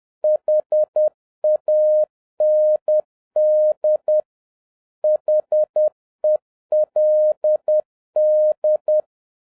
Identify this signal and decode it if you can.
Morse code, heard as a keyed tone, about 10 wpm, 610 Hz. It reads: HAND HELD